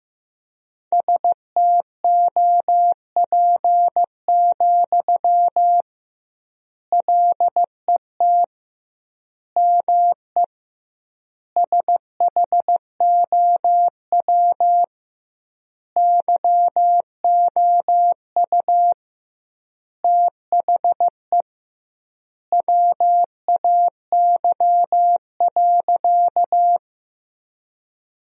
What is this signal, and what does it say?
Morse code, heard as a keyed tone, about 15 wpm, 700 Hz. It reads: STOP, LET ME SHOW YOU THE WAY.
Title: 2x Word-Spacing - Sentences from Top 300 Words - 15wpm